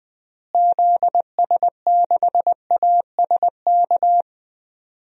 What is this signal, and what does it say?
Morse code, heard as a keyed tone, about 20 wpm, 700 Hz. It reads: ZS6ASK
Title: DXCC-100 Call Signs - 6 Characters - Repeated 3x - 20wpm